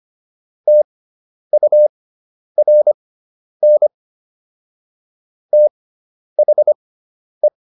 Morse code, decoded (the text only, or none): TURN THE